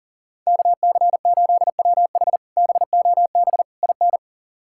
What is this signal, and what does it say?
Morse code, heard as a keyed tone, about 40 wpm, 700 Hz. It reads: KC8WH BOB IN
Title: K1USN SST QSO Practice - 40wpm